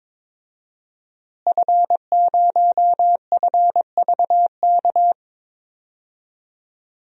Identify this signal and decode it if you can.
Morse code, heard as a keyed tone, about 21 wpm, 700 Hz. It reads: F0FVK
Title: DXCC-100 Call Signs - 5 Characters - Repeated 3x - 22wpm